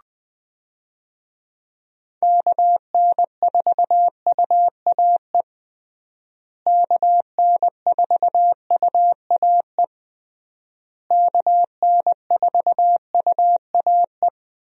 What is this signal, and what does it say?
Morse code, heard as a keyed tone, about 21 wpm, 700 Hz. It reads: KN4UAE KN4UAE KN4UAE
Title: USA Call Signs - 6 Characters - Repeated 3x - 20wpm